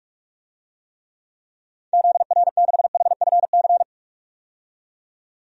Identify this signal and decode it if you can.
Morse code, heard as a keyed tone, about 41 wpm, 700 Hz. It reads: ZR6HFC
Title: DXCC-100 Call Signs - 6 Characters - 45wpm